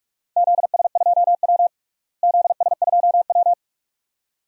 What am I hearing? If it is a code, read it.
Morse code, heard as a keyed tone, about 46 wpm, 700 Hz. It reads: ZS2W ZS2W
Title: DXCC-100 Call Signs - 4 Characters - Repeated 2x - 45wpm